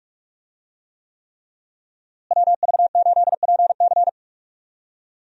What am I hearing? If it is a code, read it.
Morse code, heard as a keyed tone, about 46 wpm, 700 Hz. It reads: WV8PC
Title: USA Call Signs - 5 Characters - Repeated 3x - 45wpm